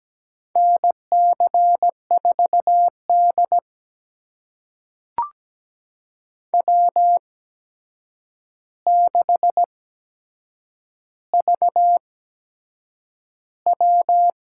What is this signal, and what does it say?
Morse code, heard as a keyed tone, about 17 wpm, spaced as for 7 wpm, 700 Hz. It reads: NC4D W6VW
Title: USA Call Signs - 4 Characters - 8x Character Spacing - 17wpm